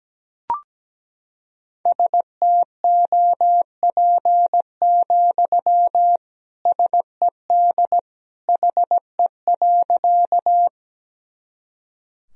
Morse code, heard as a keyed tone, about 17 wpm, 700 Hz. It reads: STOP, SED HE.
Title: Sentences from Top 300 Words and Abbreviations 17wpm